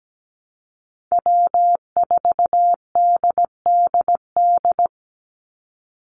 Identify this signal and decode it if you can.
Morse code, heard as a keyed tone, about 17 wpm, 700 Hz. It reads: W4DDD
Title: Call Signs Repeated 3x - 17wpm